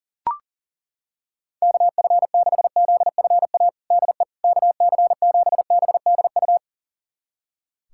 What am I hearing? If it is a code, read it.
Morse code, heard as a keyed tone, about 40 wpm, 700 Hz. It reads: KF6ZFA DE KC7BDU
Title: USA Callsign DE Callsign - 6 Characters - 40wpm